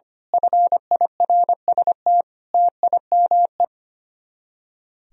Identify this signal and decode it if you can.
Morse code, heard as a keyed tone, about 24 wpm, 700 Hz. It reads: FIRST TIME
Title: Speed-Racing - Sets of 5 Words 25wpm